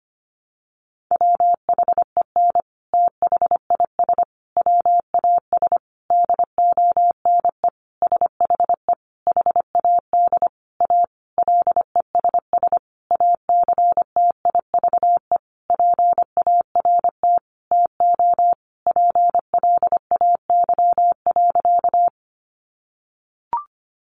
Morse code, heard as a keyed tone, about 25 wpm, 700 Hz. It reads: WHEN THIS WAS DONE SHE HAD A LESS ACTIVE PART TO PLAY.
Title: Sentences from Top 900 Words 25wpm